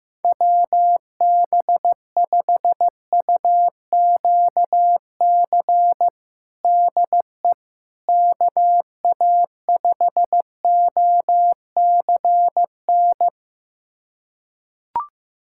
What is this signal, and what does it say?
Morse code, heard as a keyed tone, about 15 wpm, 700 Hz. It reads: WB5UQC DE KA5OCN
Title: USA Callsign DE Callsign - 6 Characters - 15wpm